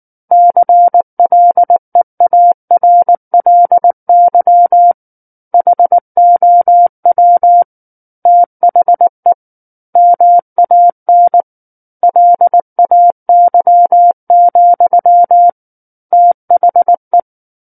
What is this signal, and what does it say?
Morse code, heard as a keyed tone, about 19 wpm, 700 Hz. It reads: CLEARLY HOW THE MAN LAY, THE
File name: Book One / 10-Chapter 10 - 19 WPM